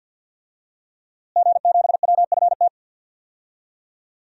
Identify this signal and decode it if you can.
Morse code, heard as a keyed tone, about 52 wpm, 700 Hz. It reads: G7WFT
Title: DXCC-100 Call Signs - 5 Characters - Repeated 2x - 50wpm